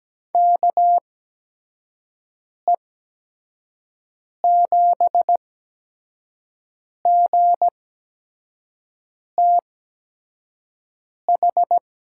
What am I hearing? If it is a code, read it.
Morse code, heard as a keyed tone, about 17 wpm, spaced as for 5 wpm, 700 Hz. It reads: KE7GTH
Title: USA Call Signs - 6 Characters - 8x Character Spacing - 17wpm